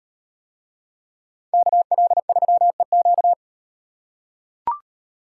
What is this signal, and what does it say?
Morse code, heard as a keyed tone, about 38 wpm, 700 Hz. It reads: KL3EQ